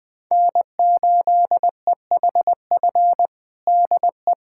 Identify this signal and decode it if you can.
Morse code, heard as a keyed tone, about 21 wpm, 700 Hz. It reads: N8EHF DE
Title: USA Callsign DE Callsign - 5 Characters - 20wpm